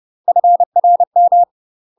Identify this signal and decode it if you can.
Morse code, heard as a keyed tone, about 30 wpm, 700 Hz. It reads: FRM
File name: UK MASTER QSOs TRAINING/30WPM